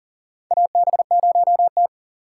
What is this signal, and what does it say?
Morse code, heard as a keyed tone, about 40 wpm, 700 Hz. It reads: AB0T
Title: USA Call Signs - 3x Character Spacing - 40wpm